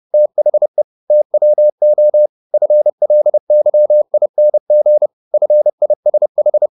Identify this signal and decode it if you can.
Morse code, heard as a keyed tone, about 30 wpm, 600 Hz. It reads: THE TWO FLYING FISH